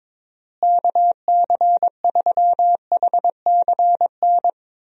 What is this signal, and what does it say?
Morse code, heard as a keyed tone, about 22 wpm, 700 Hz. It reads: KC3HCN
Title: USA Call Signs - 6 Characters - Repeated 2x - 22wpm